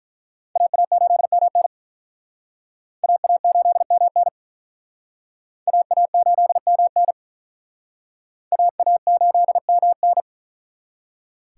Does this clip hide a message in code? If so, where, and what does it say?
Yes: Morse code from 0.6 to 10.2 s, heard as a keyed tone, about 48 wpm, 700 Hz. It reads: AA8MN AA8MN AA8MN AA8MN